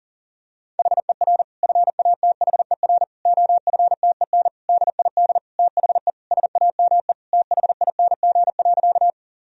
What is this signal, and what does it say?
Morse code, heard as a keyed tone, about 40 wpm, 700 Hz. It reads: HER FATHER OFTEN DID THE SAME THING.
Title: Sentences from Top 1000 Words - Review Part 1 - 40wpm